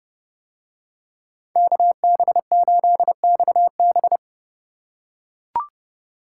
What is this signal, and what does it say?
Morse code, heard as a keyed tone, about 30 wpm, 700 Hz. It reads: KB8XB